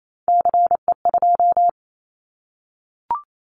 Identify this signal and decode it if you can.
Morse code, heard as a keyed tone, about 28 wpm, 700 Hz. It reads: CE2
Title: Callsign Prefixes for Top 100 DXCC - 28wpm